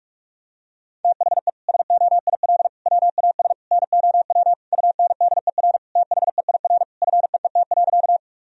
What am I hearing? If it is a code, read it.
Morse code, heard as a keyed tone, about 46 wpm, 700 Hz. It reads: THE SOIL WAS NOW UNDER THEIR FEET.
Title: Sentences from Top 1000 Words - Review Part 4 - 45wpm